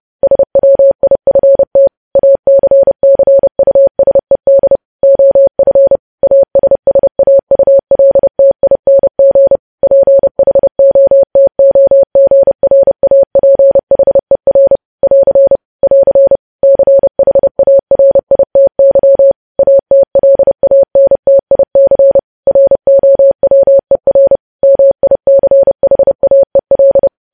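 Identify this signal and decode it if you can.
Morse code, heard as a keyed tone, about 30 wpm, 550 Hz. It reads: SWIFT ACCUSED OF ASSAULTING PHOTOGRAPHER <AR> <AR> CHARITY ATLANTIC ROWER MICHAEL
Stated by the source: News headlines sent in Morse at 30 WPM.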